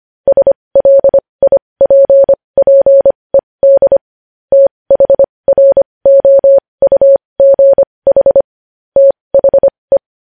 Morse code, heard as a keyed tone, about 25 wpm, spaced as for 20 wpm, 550 Hz. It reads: SLIPPED THROUGH THE
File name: podcast / News headlines-1x-25x20